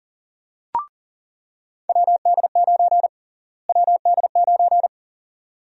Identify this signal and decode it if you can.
Morse code, heard as a keyed tone, about 40 wpm, 700 Hz. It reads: WD9 WD9
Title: USA Call Sign Prefixes - Repeated 2x - 40wpm